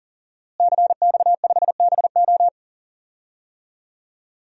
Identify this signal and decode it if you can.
Morse code, heard as a keyed tone, about 40 wpm, 700 Hz. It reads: CX5BO
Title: DXCC-100 Call Signs - 5 Characters - Repeated 3x - 40wpm